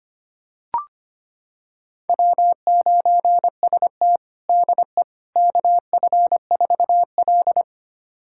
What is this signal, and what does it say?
Morse code, heard as a keyed tone, about 25 wpm, 700 Hz. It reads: W9ST DE KF4L
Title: USA Callsign DE Callsign - 4 Characters - 25wpm